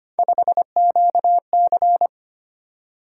Morse code, heard as a keyed tone, about 25 wpm, 700 Hz. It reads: 5QC
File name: Three Letter-Number Combinations - 25wpm